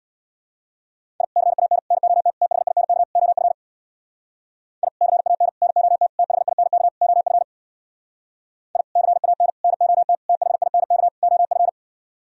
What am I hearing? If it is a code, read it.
Morse code, heard as a keyed tone, about 64 wpm, 700 Hz. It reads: I CAN NOT THINK OF I CAN NOT THINK OF I CAN NOT THINK OF